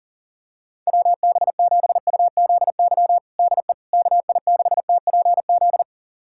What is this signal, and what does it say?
Morse code, heard as a keyed tone, about 40 wpm, 690 Hz. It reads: WB7UZY DE KI6TPZ
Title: USA Callsign DE Callsign - 6 Characters - 40wpm